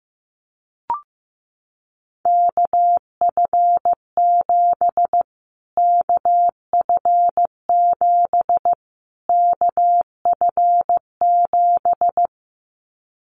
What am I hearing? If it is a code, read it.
Morse code, heard as a keyed tone, about 15 wpm, 700 Hz. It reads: KF7 KF7 KF7
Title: Callsign Prefixes for Top 25 DXCC - Repeated 3x - 15wpm